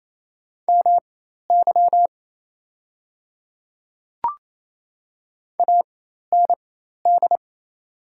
Morse code, heard as a keyed tone, about 28 wpm, spaced as for 13 wpm, 690 Hz. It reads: MY AND
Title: 4x Character Spacing - Top 200 Words - 28wpm